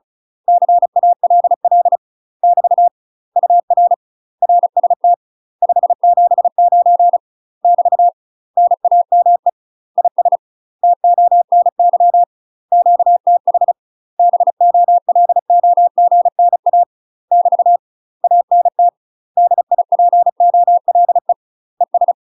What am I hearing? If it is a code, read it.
Morse code, heard as a keyed tone, about 36 wpm, 700 Hz. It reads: CALL <BT> UR RST 579 <BT> NAME IS TONY QTH BOLOGNA <BT> ANT DIPOLE ES
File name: UK MASTER QSOs TRAINING/35WPM